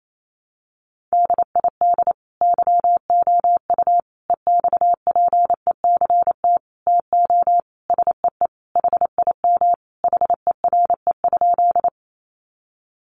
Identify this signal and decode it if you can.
Morse code, heard as a keyed tone, about 28 wpm, 700 Hz. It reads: DID YOU EXPECT TO SEE HIM HERE?
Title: Sentences from Top 1100 Words 28wpm